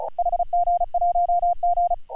Morse code, heard as a keyed tone, about 39 wpm, 690 Hz. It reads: HG1G
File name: AUDIO2019 / HG1G CW